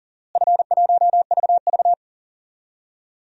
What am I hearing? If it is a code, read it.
Morse code, heard as a keyed tone, about 40 wpm, 700 Hz. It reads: F1VV
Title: DXCC-100 Call Signs - 4 Characters - Repeated 2x - 40wpm